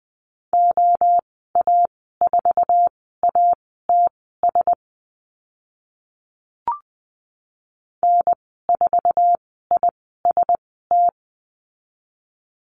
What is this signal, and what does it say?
Morse code, heard as a keyed tone, about 20 wpm, spaced as for 14 wpm, 700 Hz. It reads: OA4ATS N4IST
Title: 062 Lesson 11g - Letter h - Call Signs 20wpm